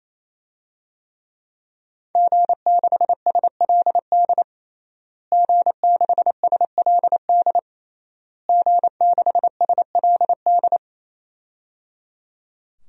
Morse code, heard as a keyed tone, about 27 wpm, 700 Hz. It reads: G6SLD G6SLD G6SLD
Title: DXCC-100 Call Signs - 5 Characters - Repeated 3x - 28wpm